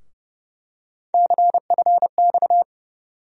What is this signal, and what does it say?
Morse code, heard as a keyed tone, about 30 wpm, 700 Hz. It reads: CFX
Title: USA Call Sign Postfixes - 3 Letters 30wpm